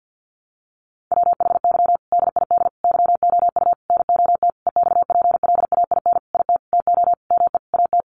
Morse code, heard as a keyed tone, about 50 wpm, 700 Hz. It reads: WHY DID YOU NOT EXPLAIN IT TO ME AT
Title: Sentences from Top 1100 Words 50wpm